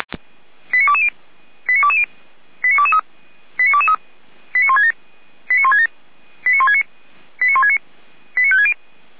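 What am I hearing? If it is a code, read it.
Five-tone selective call (ZVEI), heard as quick runs of five tones, tones of about 70 ms. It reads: 89220 89220 89223 89223 89167 89167 89169 89169 89560
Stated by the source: ZVEI five-tone selective call received by SDR.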